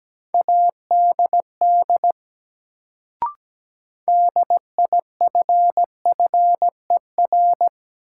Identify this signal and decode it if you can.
Morse code, heard as a keyed tone, about 17 wpm, 700 Hz. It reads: ADD DIFFER
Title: Sentences from Top 1000 Words - Review Part 1 - New Word Drill - 17wpm